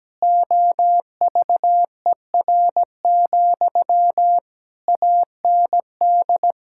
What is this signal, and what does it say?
Morse code, heard as a keyed tone, about 17 wpm, 700 Hz. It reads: OVER, AND
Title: Sentences from Top 1600 Words 17wpm